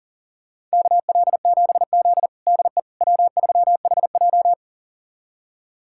Morse code, heard as a keyed tone, about 40 wpm, 700 Hz. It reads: KL7Z DE W3HJ
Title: USA Callsign DE Callsign - 4 Characters - 40wpm